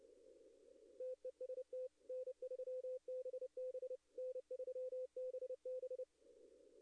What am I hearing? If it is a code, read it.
Morse code, heard as a keyed tone, about 30 wpm, 510 Hz. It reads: TEST N3BB N3BB
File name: contests / arrl-160-k5tr-op-w5tn -2300